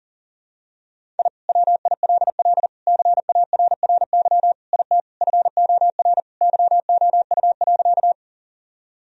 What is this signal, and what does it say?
Morse code, heard as a keyed tone, about 40 wpm, 700 Hz. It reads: I WILL CARRY IT FOR YOU.
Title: Sentences from Top 1000 Words - Review Part 1 - 40wpm